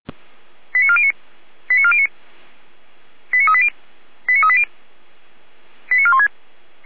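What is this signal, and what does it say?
Five-tone selective call (ZVEI), heard as quick runs of five tones, tones of about 70 ms. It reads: 89409 89409 89380 89380 89416
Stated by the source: ZVEI five-tone call 89340 received by SDR.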